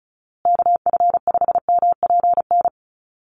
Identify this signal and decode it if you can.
Morse code, heard as a keyed tone, about 34 wpm, 700 Hz. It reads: KF5MPN
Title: USA Call Signs Repeated 2x 35wpm